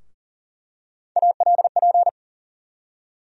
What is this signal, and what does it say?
Morse code, heard as a keyed tone, about 40 wpm, 700 Hz. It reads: ALP
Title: USA Call Sign Postfixes - 3 Letters 40wpm